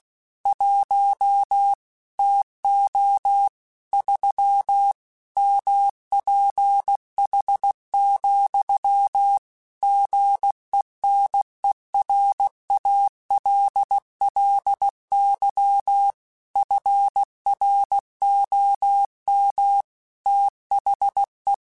Morse code, heard as a keyed tone, about 16 wpm, 800 Hz. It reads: 1 TO 3 MPH, GENERALLY FROM THE
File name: Weather Sets / Weather Sets-mp3s / Weather 4 28 @15